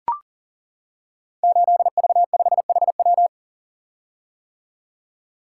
Morse code, heard as a keyed tone, about 42 wpm, 700 Hz. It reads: 8V5HW
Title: Code Groups - 5 Letter:Numbers - 40wpm